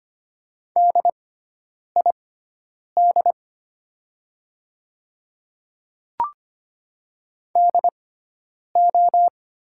Morse code, heard as a keyed tone, about 25 wpm, spaced as for 9 wpm, 700 Hz. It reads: DID DO
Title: 6x Character Spacing - Top 100 Words - 25wpm